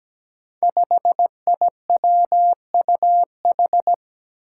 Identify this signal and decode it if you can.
Morse code, heard as a keyed tone, about 17 wpm, 700 Hz. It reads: 5IWUH
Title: Five Challenging Letter-Number Combinations 17wpm